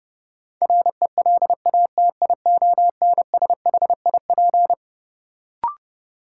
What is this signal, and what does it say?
Morse code, heard as a keyed tone, about 30 wpm, 700 Hz. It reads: RELATIONSHIP
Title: Sentences from Top 1100 Words - New Word Drill Alt - 30wpm